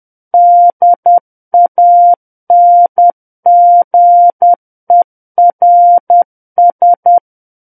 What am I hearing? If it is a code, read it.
Morse code, heard as a keyed tone, about 10 wpm, 700 Hz. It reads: DANGERS
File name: Book One / 01-Chapter 1 - 10 WPM